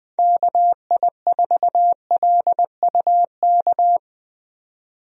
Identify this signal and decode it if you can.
Morse code, heard as a keyed tone, about 20 wpm, 700 Hz. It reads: KI4LUK
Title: USA Call Signs - 6 Characters - 20wpm